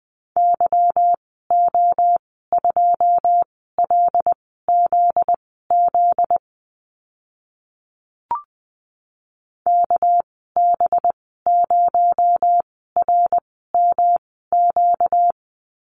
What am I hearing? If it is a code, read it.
Morse code, heard as a keyed tone, about 20 wpm, spaced as for 14 wpm, 700 Hz. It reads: YO2LZZ KB0RMQ
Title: 245 Lesson 39g - Letter z - Call Signs 20wpm